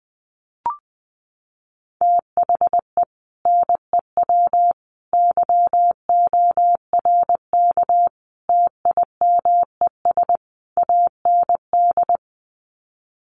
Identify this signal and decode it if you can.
Morse code, heard as a keyed tone, about 20 wpm, 700 Hz. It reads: THE NEW YORK TIMES AND
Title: Sets of 5 Words Encore 20wpm